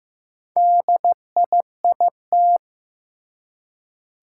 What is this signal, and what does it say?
Morse code, heard as a keyed tone, about 15 wpm, 700 Hz. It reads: DIIT
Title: Four Challenging Letter-Number Combinations 15wpm